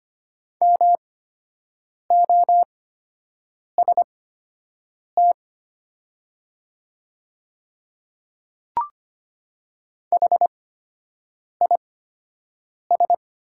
Morse code, heard as a keyed tone, about 25 wpm, spaced as for 7 wpm, 700 Hz. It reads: MOST HIS
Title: 8x Character Spacing - Top 100 Words - 25wpm